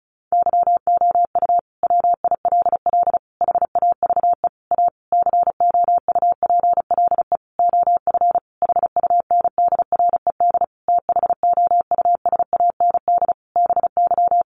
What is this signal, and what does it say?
Morse code, heard as a keyed tone, about 34 wpm, 700 Hz. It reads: YOU WILL HAVE A COUPLE OF HUNDRED THOUSAND BY
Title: Sentences from Top 1100 Words 35wpm